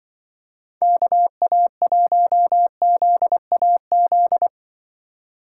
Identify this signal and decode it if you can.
Morse code, heard as a keyed tone, about 23 wpm, 700 Hz. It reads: KA1ZAZ
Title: Speed Racing - USA Call Signs 20wpm